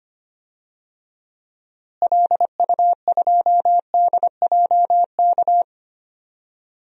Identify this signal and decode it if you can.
Morse code, heard as a keyed tone, about 25 wpm, 700 Hz. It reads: LU2DJK
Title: DXCC-100 Call Signs - 6 Characters - Repeated 3x - 25wpm